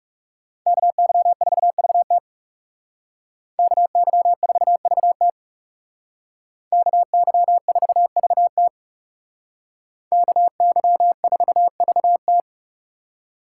Morse code, heard as a keyed tone, about 35 wpm, 700 Hz. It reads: KY4VT KY4VT KY4VT KY4VT
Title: Speed Racing - USA 2x2 Call Signs 30wpm